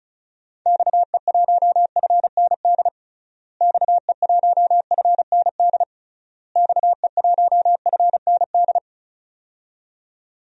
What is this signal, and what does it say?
Morse code, heard as a keyed tone, about 36 wpm, 700 Hz. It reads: XE1FND XE1FND XE1FND
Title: DXCC-100 Call Signs - 6 Characters - Repeated 3x - 35wpm